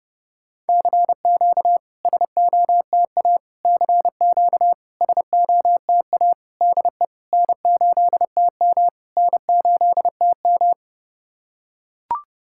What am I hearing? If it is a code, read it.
Morse code, heard as a keyed tone, about 30 wpm, 690 Hz. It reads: CQ SOTA CQ SOTA DE N8TM N8TM
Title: SOTA QSO Practice 30wpm